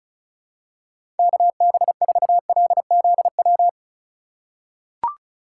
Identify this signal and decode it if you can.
Morse code, heard as a keyed tone, about 36 wpm, 700 Hz. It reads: KB4LZW
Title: USA Call Signs - 6 Characters - 2x Character Spacing - 35wpm